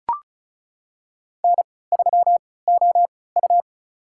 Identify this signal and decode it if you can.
Morse code, heard as a keyed tone, about 36 wpm, spaced as for 20 wpm, 700 Hz. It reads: N3OU
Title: USA Call Signs - 4 Characters - 3x Character Spacing - 35wpm